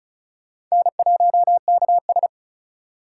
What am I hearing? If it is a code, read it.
Morse code, heard as a keyed tone, about 34 wpm, 700 Hz. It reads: N1KS
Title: USA Call Signs - 4 Characters - 4x Character Spacing - 35wpm